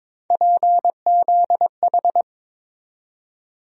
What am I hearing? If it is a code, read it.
Morse code, heard as a keyed tone, about 23 wpm, 700 Hz. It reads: PZH
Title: USA Call Sign Postfixes - 3 Letters 22wpm